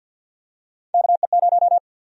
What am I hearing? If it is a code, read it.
Morse code, heard as a keyed tone, about 52 wpm, 700 Hz. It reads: KE0
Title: USA Call Sign Prefixes - 50wpm